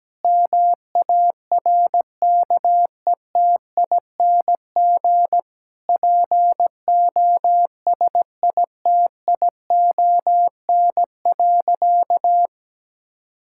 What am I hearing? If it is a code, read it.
Morse code, heard as a keyed tone, about 17 wpm, 700 Hz. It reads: MARKETING POSITION.
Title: Sentences from Top 1000 Words - Review Part 5 - 17wpm